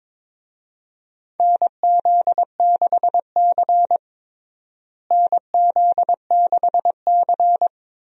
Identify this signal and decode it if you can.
Morse code, heard as a keyed tone, about 22 wpm, 700 Hz. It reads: NZ6C NZ6C
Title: USA Call Signs - 4 Characters - Repeated 2x - 22wpm